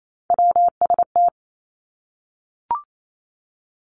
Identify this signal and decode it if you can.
Morse code, heard as a keyed tone, about 28 wpm, 700 Hz. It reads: WST
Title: Three Challenging Letter-Number Combinations 28wpm